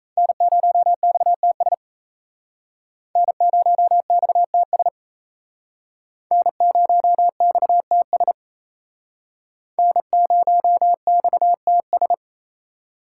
Morse code, heard as a keyed tone, about 36 wpm, 700 Hz. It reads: N0XTS N0XTS N0XTS N0XTS